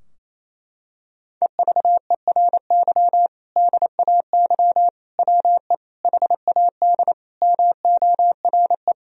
Morse code, heard as a keyed tone, about 27 wpm, 700 Hz. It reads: EVERY DAY WE HAD MORE
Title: Sentences from Top 1000 Words - Review Part 3 - 28wpm